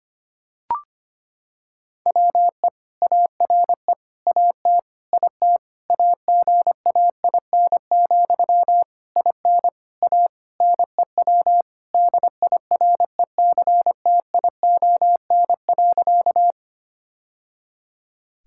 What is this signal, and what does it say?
Morse code, heard as a keyed tone, about 25 wpm, 700 Hz. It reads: WE ARE AT IT AGAIN, IN A NEW DIRECTION.
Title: Sentences from Top 1000 Words - Review Part 5 - 25wpm